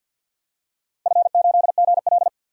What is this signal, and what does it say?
Morse code, heard as a keyed tone, about 47 wpm, 700 Hz. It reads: F8GL
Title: DXCC-100 Call Signs - 4 Characters - 50wpm